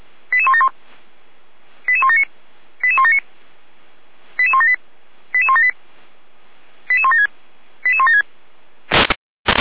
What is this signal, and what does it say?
Five-tone selective call (ZVEI), heard as quick runs of five tones, tones of about 70 ms. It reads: 88171 88179 88179 88178 88178 88176 88176
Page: SDR/ZVEI